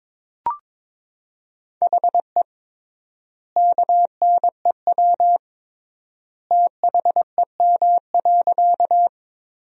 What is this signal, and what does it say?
Morse code, heard as a keyed tone, about 22 wpm, 700 Hz. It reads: HE KNEW THEM.
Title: 3x Word-Spacing - Sentences from Top 400 Words - 22wpm